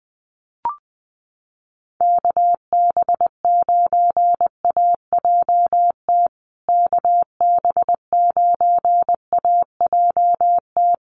Morse code, heard as a keyed tone, about 19 wpm, 700 Hz. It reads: KB9AJT KB9AJT
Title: USA Call Signs Repeated 3x 20wpm